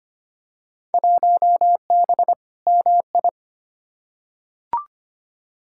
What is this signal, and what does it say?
Morse code, heard as a keyed tone, about 25 wpm, 700 Hz. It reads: 1B MI
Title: Field Day Exchanges - 25wpm